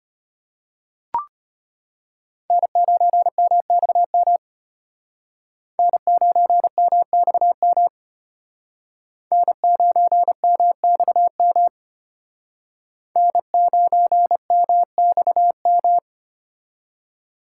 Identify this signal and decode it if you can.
Morse code, heard as a keyed tone, about 32 wpm, 700 Hz. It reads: N9MXM N9MXM N9MXM N9MXM